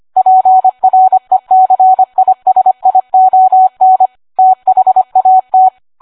Morse code, heard as a keyed tone, about 25 wpm, 750 Hz. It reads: PRECISION THAT